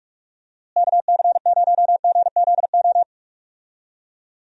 Morse code, heard as a keyed tone, about 44 wpm, 700 Hz. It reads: KC0GZO
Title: USA Call Signs - 6 Characters - Repeated 2x - 45wpm